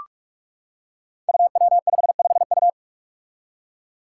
Morse code, heard as a keyed tone, about 46 wpm, 700 Hz. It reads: UW55U
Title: Five Challenging Letter-Number Combinations 45wpm